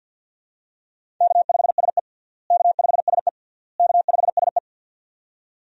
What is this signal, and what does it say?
Morse code, heard as a keyed tone, about 48 wpm, 700 Hz. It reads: K5SE K5SE K5SE
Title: USA Call Signs - 4 Characters - Repeated 3x - 50wpm